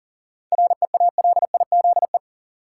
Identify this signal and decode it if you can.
Morse code, heard as a keyed tone, about 40 wpm, 700 Hz. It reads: REALIZE
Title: Seven Letter Words - 40wpm